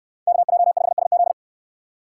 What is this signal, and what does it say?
Morse code, heard as a keyed tone, about 68 wpm, 700 Hz. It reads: KJ5SZ